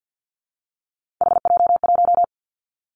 Morse code, heard as a keyed tone, about 49 wpm, 700 Hz. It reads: HJ1
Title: Three Challenging Letter-Number Combinations 50wpm